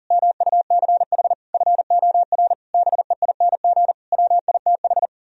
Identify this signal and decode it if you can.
Morse code, heard as a keyed tone, about 48 wpm, 700 Hz. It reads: MUCH FOR BEING WITH
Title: Speed-Racing - Sets of 5 Words 40wpm